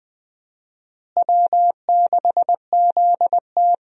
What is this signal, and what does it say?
Morse code, heard as a keyed tone, about 20 wpm, 700 Hz. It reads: W6ZT
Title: USA Call Signs - 4 Characters - Repeated 2x - 20wpm